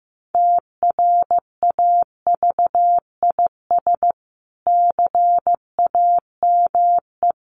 Morse code, heard as a keyed tone, about 14 wpm, 700 Hz. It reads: TRAVIS CAME
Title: A Sound Of Thunder 15wpm